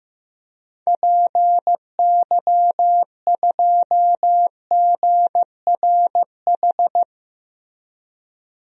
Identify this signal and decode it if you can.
Morse code, heard as a keyed tone, about 15 wpm, 700 Hz. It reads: PY2GRH
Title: DXCC-100 Call Signs - 6 Characters - Repeated 2x - 15wpm